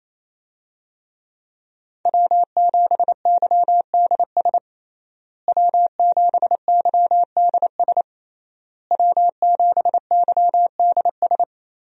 Morse code, heard as a keyed tone, about 28 wpm, 700 Hz. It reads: W7YDS W7YDS W7YDS
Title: USA Call Signs - 5 Characters - Repeated 3x - 28wpm